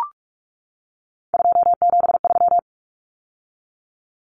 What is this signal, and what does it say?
Morse code, heard as a keyed tone, about 44 wpm, 700 Hz. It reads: 273
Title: Numbers - 3-Digit 45wpm